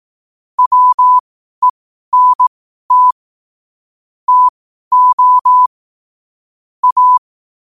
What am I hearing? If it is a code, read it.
Morse code, heard as a keyed tone, about 18 wpm, spaced as for 12 wpm, 1000 Hz. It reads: WENT TO A
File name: qotdfiles / 12wpm